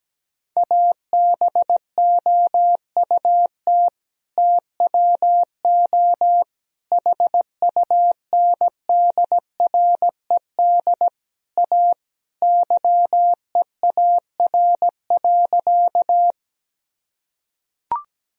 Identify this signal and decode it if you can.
Morse code, heard as a keyed tone, about 17 wpm, 700 Hz. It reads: ABOUT TWO HUNDRED A YEAR.
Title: Sentences from Top 1000 Words - Review Part 2 - 17wpm